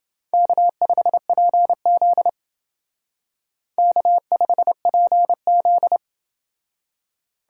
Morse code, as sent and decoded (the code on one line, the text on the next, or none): -.- ..... .--. --.. / -.- ..... .--. --..
K5PZ K5PZ